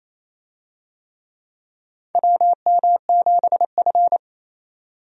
Morse code, heard as a keyed tone, about 28 wpm, 700 Hz. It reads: WM7F
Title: USA Call Signs - 4 Characters - 28wpm